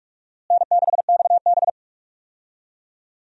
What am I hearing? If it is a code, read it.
Morse code, heard as a keyed tone, about 46 wpm, 700 Hz. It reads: N6XB
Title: USA Call Signs - 4 Characters - Repeated 2x - 45wpm